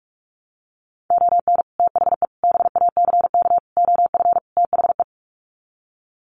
Morse code, heard as a keyed tone, about 44 wpm, 700 Hz. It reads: ON THE BACK OF THE
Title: Sets of 5 Words 45wpm